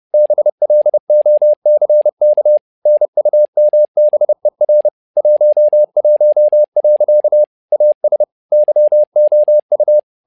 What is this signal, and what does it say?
Morse code, heard as a keyed tone, about 30 wpm, 600 Hz. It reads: BLOCK NUMBER 11. AS YOU
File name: learning center / tower trouble / tower trouble 30wpm